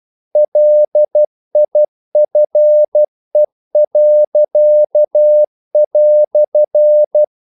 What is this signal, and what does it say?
Morse code, heard as a keyed tone, about 12 wpm, 600 Hz. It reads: LIFE."
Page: English quotations in CW (telegraphy - morse)